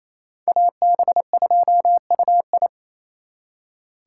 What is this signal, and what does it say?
Morse code, heard as a keyed tone, about 35 wpm, 700 Hz. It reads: AB2UI